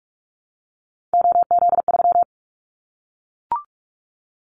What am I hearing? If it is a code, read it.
Morse code, heard as a keyed tone, about 44 wpm, 700 Hz. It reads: OZ3
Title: Callsign Prefixes for Top 50 DXCC - 45wpm